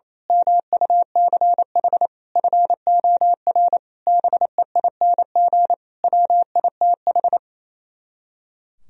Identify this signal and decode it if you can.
Morse code, heard as a keyed tone, about 27 wpm, 700 Hz. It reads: MUCH FOR BEING WITH
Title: Speed-Racing - Sets of 5 Words 28wpm